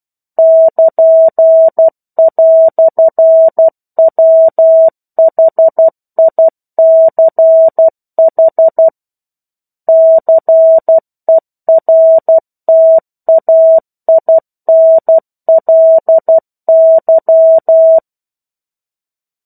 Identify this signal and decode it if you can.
Morse code, heard as a keyed tone, about 12 wpm, 650 Hz. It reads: <KN>"WHICH CERTAINLY
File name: 01-Chapter-01 12 WPM